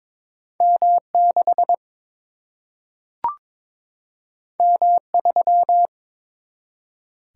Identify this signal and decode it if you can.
Morse code, heard as a keyed tone, about 23 wpm, 700 Hz. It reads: M 6 M 3
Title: Single-Letter Single-Number 22wpm